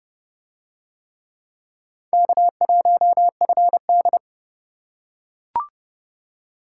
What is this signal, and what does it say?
Morse code, heard as a keyed tone, about 29 wpm, 700 Hz. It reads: K1FD